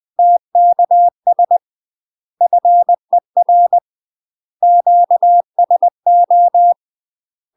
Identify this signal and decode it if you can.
Morse code, heard as a keyed tone, about 20 wpm, 700 Hz. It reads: TKS FER QSO
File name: UK MASTER QSOs TRAINING / 20WPM / UK MASTER QSOs 0023